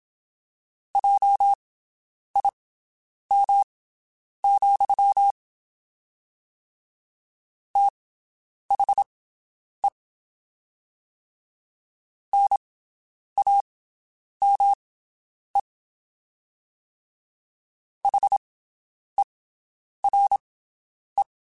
Morse code, heard as a keyed tone, about 27 wpm, spaced as for 9 wpm, 800 Hz. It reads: JIM, THE NAME HERE
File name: (spaced) / QSOs Spaced-mp3s / WD0BGP 1 @25 sp